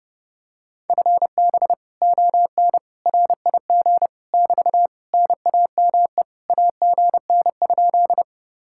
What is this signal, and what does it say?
Morse code, heard as a keyed tone, about 32 wpm, 700 Hz. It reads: FB ON RIG <BT> NAME AGN?